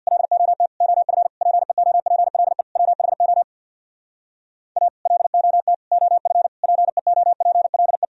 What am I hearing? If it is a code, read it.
Morse code, heard as a keyed tone, about 62 wpm, 700 Hz. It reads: LOT OF PEOPLE WHO A LOT OF PEOPLE
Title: Speed-Racing - Sets of 5 Words 50wpm